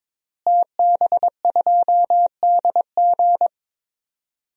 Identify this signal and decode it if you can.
Morse code, heard as a keyed tone, about 22 wpm, 700 Hz. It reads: TB2DG
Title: DXCC-100 Call Signs - 5 Characters - Repeated 2x - 22wpm